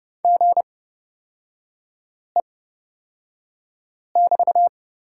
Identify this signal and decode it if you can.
Morse code, heard as a keyed tone, about 30 wpm, spaced as for 5 wpm, 700 Hz. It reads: GE<BT>
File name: Lesson 36e - BT - Three Characters 30wpm